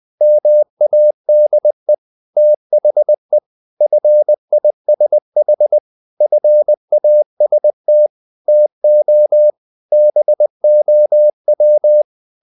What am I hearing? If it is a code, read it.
Morse code, heard as a keyed tone, about 20 wpm, 600 Hz. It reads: MADE THE FISH FAST TO BOW
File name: the old man and the sea part 2 20wpm